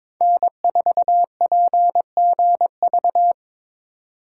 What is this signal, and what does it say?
Morse code, heard as a keyed tone, about 25 wpm, 700 Hz. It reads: N4PGV
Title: Speed Racing - USA 1x3 Call Signs 22wpm